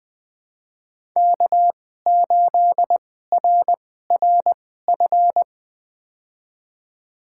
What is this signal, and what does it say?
Morse code, heard as a keyed tone, about 20 wpm, spaced as for 15 wpm, 700 Hz. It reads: K8RRF
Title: USA Call Signs - 5 Characters - 2x Character Spacing - 20wpm